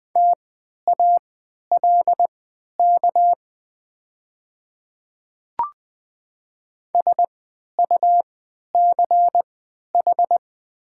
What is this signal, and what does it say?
Morse code, heard as a keyed tone, about 20 wpm, spaced as for 11 wpm, 700 Hz. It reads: TALK SUCH
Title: 3x Character Spacing - Top 500 Words - 20wpm